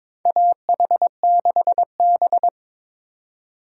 Morse code, heard as a keyed tone, about 22 wpm, 700 Hz. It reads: AH6B
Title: USA Call Signs - 4 Characters - 4x Character Spacing - 22wpm